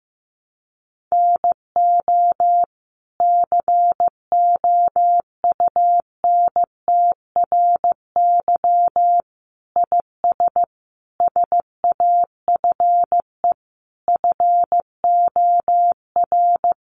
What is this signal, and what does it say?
Morse code, heard as a keyed tone, about 15 wpm, 700 Hz. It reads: NO COUNTRY IS SAFE FOR
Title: Sentences from Top 900 Words 15wpm